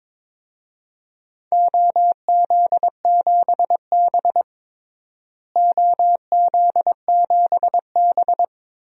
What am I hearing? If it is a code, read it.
Morse code, heard as a keyed tone, about 22 wpm, 700 Hz. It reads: OZ7B OZ7B
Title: DXCC-100 Call Signs - 4 Characters - Repeated 3x - 22wpm